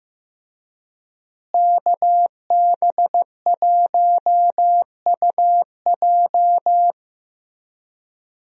Morse code, heard as a keyed tone, about 15 wpm, 700 Hz. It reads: KB1UJ
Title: USA Call Signs - 5 Characters - Repeated 3x - 15wpm